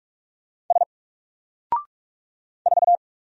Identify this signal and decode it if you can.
Morse code, heard as a keyed tone, about 44 wpm, 700 Hz. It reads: S 4
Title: Single Letter-Number 45wpm